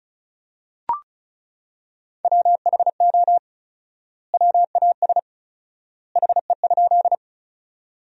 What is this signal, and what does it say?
Morse code, heard as a keyed tone, about 36 wpm, 700 Hz. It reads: WHO WAS HE?
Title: 4x Word-Spacing - Sentences from Top 100 Words - 35wpm